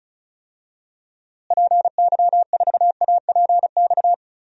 Morse code, heard as a keyed tone, about 36 wpm, 700 Hz. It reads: PY4APX
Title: DXCC-100 Call Signs - 6 Characters - Repeated 3x - 35wpm